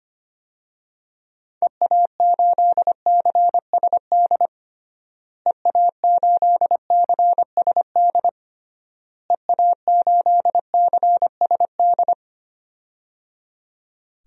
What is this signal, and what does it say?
Morse code, heard as a keyed tone, about 25 wpm, 700 Hz. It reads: EA8CSD EA8CSD EA8CSD
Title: DXCC-100 Call Signs - 6 Characters - Repeated 3x - 25wpm